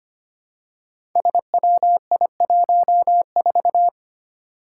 Morse code, heard as a keyed tone, about 25 wpm, 700 Hz. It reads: SWI14
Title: Five Challenging Letter-Number Combinations 25wpm